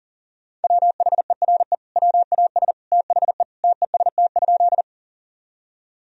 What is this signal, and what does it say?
Morse code, heard as a keyed tone, about 40 wpm, 700 Hz. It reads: WHERE WAS THE TEST?
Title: Sentences from Top 1000 Words - Review Part 3 - 40wpm